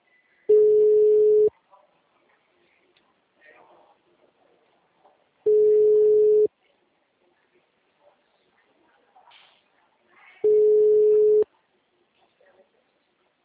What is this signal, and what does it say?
Ringback tone (Europe, 425 Hz), with speech.